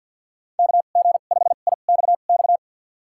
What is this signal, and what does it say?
Morse code, heard as a keyed tone, about 51 wpm, spaced as for 36 wpm, 700 Hz. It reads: KG5IXX
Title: USA Call Signs - 2x Character Spacing - 50wpm